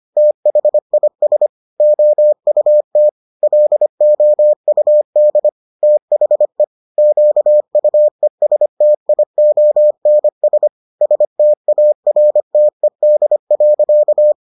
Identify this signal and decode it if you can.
Morse code, heard as a keyed tone, about 25 wpm, 600 Hz. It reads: THIS OUT LOUD THE QUESTIONS STARTED.